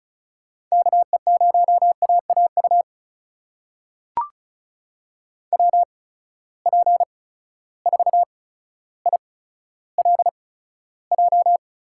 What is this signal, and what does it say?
Morse code, heard as a keyed tone, about 34 wpm, spaced as for 15 wpm, 700 Hz. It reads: KE0AAU WP4ILJ
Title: USA Call Signs - 6 Characters - 8x Character Spacing - 35wpm